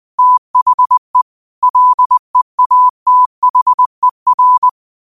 Morse code, heard as a keyed tone, about 20 wpm, 1000 Hz. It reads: THE LEATHER
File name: 20wpm